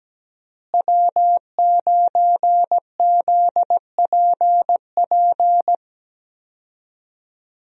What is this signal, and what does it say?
Morse code, heard as a keyed tone, about 17 wpm, 700 Hz. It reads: W9ZPP
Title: USA Call Signs - 5 Characters - Repeated 3x - 17wpm